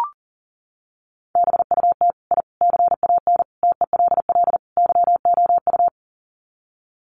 Morse code, heard as a keyed tone, about 40 wpm, 700 Hz. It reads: BUT I CAN TELL YOU
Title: Sets of 5 Words Encore 40wpm